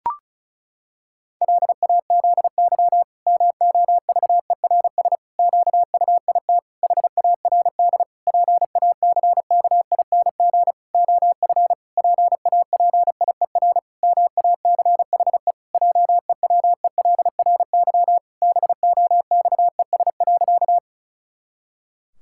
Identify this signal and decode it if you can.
Morse code, heard as a keyed tone, about 36 wpm, 700 Hz. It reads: LAZY MOVERS QUIT HARD PACKING OF PAPIER MACHE JEWELRY BOXES.
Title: Pangram Challenge v2 - 35wpm